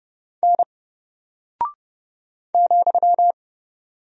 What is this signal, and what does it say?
Morse code, heard as a keyed tone, about 30 wpm, 700 Hz. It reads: N ,
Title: Single Character (Letter:Number:Punctuation:Prosign) Alt 30wpm